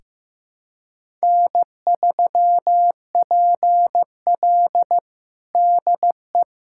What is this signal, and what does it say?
Morse code, heard as a keyed tone, about 15 wpm, 700 Hz. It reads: N3PL DE
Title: USA Callsign DE Callsign - 4 Characters - 15wpm